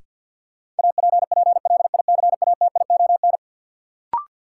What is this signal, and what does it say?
Morse code, heard as a keyed tone, about 47 wpm, 700 Hz. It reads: APPLICATION
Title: Sentences from Top 1200 Words - New Word Drill Alt - 50wpm